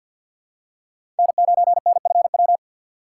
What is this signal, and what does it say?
Morse code, heard as a keyed tone, about 50 wpm, 700 Hz. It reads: N9NFW
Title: USA Call Signs - 5 Characters - 3x Character Spacing - 50wpm